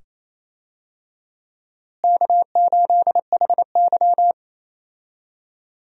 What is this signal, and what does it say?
Morse code, heard as a keyed tone, about 28 wpm, 700 Hz. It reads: K8HY
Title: USA Call Signs - 4 Characters - Repeated 2x - 28wpm